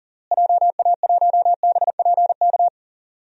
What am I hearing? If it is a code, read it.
Morse code, heard as a keyed tone, about 40 wpm, 700 Hz. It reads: JA1BPK
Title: DXCC-100 Call Signs - 6 Characters - Repeated 3x - 40wpm